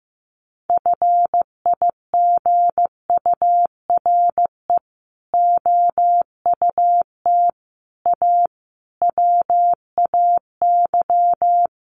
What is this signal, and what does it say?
Morse code, heard as a keyed tone, about 15 wpm, 700 Hz. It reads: FIGURE OUT A WAY
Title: Sets of 5 Words Encore2 15wpm